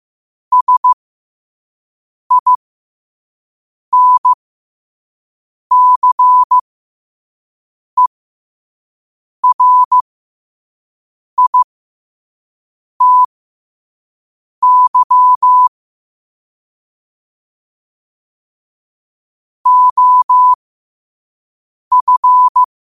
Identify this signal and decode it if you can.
Morse code, heard as a keyed tone, about 15 wpm, spaced as for 5 wpm, 1000 Hz. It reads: SINCERITY OF
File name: qotdfiles / 5wpm